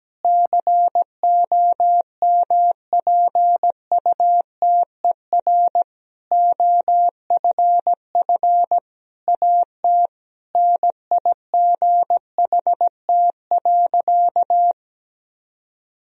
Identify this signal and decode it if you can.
Morse code, heard as a keyed tone, about 17 wpm, 700 Hz. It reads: COMPUTER OFF AT NIGHT.
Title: Sentences from Top 1000 Words - Review Part 3 - 17wpm